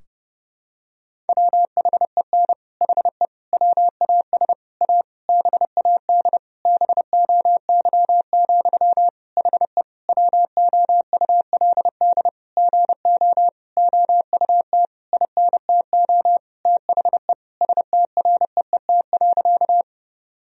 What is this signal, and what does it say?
Morse code, heard as a keyed tone, about 30 wpm, 700 Hz. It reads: WHEN HE WAS A BAD BOY, HE WOULD GO OUT INTO THE STREET.
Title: Sentences from Top 1000 Words - Review Part 3 - 30wpm